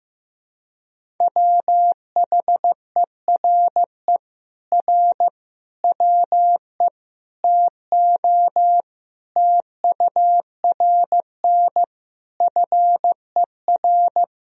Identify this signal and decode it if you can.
Morse code, heard as a keyed tone, about 15 wpm, 700 Hz. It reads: WHERE R WE TO TURN FER
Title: Sentences from Top 200 Words and Abbreviations 15wpm